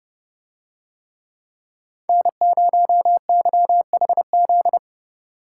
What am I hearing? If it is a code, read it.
Morse code, heard as a keyed tone, about 30 wpm, 700 Hz. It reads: N0YHZ